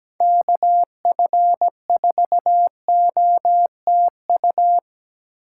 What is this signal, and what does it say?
Morse code, heard as a keyed tone, about 17 wpm, 700 Hz. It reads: KF4OTU
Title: USA Call Signs - 6 Characters - 4x Character Spacing - 17wpm